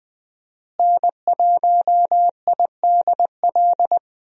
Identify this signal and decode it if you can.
Morse code, heard as a keyed tone, about 20 wpm, 700 Hz. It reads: N1IDL
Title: USA Call Signs - 5 Characters - Repeated 2x - 20wpm